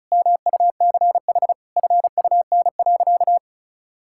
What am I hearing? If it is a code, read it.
Morse code, heard as a keyed tone, about 34 wpm, 700 Hz. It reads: MUCH FUN.
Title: Sentences from Top 1600 Words 35wpm